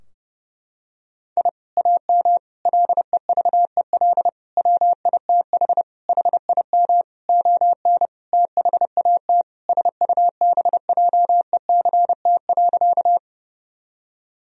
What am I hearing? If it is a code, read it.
Morse code, heard as a keyed tone, about 30 wpm, 700 Hz. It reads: I AM LEVEL WITH HIM ON THAT SUBJECT.
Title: Sentences from Top 1000 Words - Review Part 4 - 30wpm